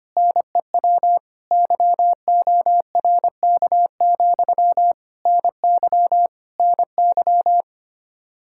Morse code, heard as a keyed tone, about 25 wpm, 700 Hz. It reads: NEW YORK, NY NY
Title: Practice QSO - Exchange 1 - Beginning - 25wpm